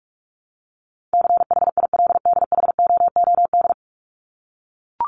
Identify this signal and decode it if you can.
Morse code, heard as a keyed tone, about 44 wpm, 700 Hz. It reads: CHILDHOOD
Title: Top 1751-2000 Words 45wpm